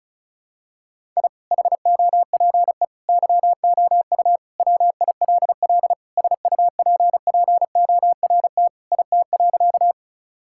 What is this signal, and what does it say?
Morse code, heard as a keyed tone, about 36 wpm, 700 Hz. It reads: I HOPE YOU WILL SUPPORT IT.
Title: Sentences from Top 1000 Words - Review Part 5 - 35wpm